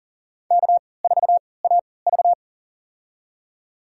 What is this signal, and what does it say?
Morse code, heard as a keyed tone, about 40 wpm, spaced as for 23 wpm, 700 Hz. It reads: K4AV
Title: USA Call Signs - 4 Characters - 3x Character Spacing - 40wpm